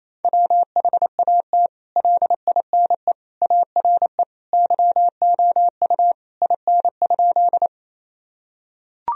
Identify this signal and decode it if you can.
Morse code, heard as a keyed tone, about 28 wpm, 700 Hz. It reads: WHAT LINE ARE YOU IN?
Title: Sentences from Top 1000 Words - Review Part 1 - 28wpm